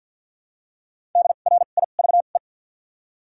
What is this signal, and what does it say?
Morse code, heard as a keyed tone, about 50 wpm, spaced as for 36 wpm, 700 Hz. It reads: DRIVE
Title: 2x Character Spacing - Top 500 Words - 50wpm